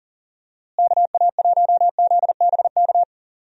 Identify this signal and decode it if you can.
Morse code, heard as a keyed tone, about 40 wpm, 700 Hz. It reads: KA1ZBK
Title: USA Call Signs - 6 Characters - Repeated 3x - 40wpm